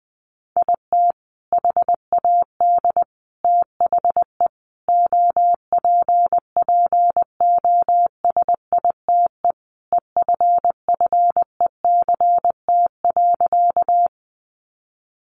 Morse code, heard as a keyed tone, about 20 wpm, 700 Hz. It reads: IT HAD THE OPPOSITE EFFECT.
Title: Sentences from Top 1300 Words 20wpm